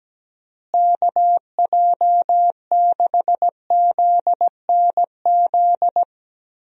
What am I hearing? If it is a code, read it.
Morse code, heard as a keyed tone, about 17 wpm, 700 Hz. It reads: KJ6ZNZ
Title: USA Call Signs - 6 Characters - Repeated 3x - 17wpm